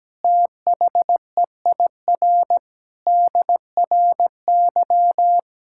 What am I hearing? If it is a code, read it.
Morse code, heard as a keyed tone, about 17 wpm, 700 Hz. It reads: THEIR DRY
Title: Sentences from Top 1600 Words 17wpm